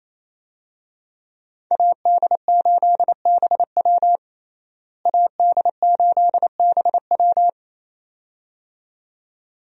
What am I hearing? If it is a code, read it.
Morse code, heard as a keyed tone, about 28 wpm, 700 Hz. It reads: AD8BW AD8BW
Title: USA Call Signs - 5 Characters - Repeated 2x - 28wpm